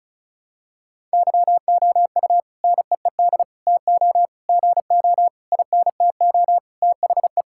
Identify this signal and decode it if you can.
Morse code, heard as a keyed tone, about 34 wpm, 700 Hz. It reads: YOU NEED TO GO INTO THE
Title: Sentences from Top 1000 Words - Review Part 6 - 35wpm